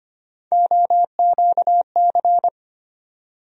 Morse code, heard as a keyed tone, about 25 wpm, 700 Hz. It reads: OQC
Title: USA Call Sign Postfixes - 3 Letters 25wpm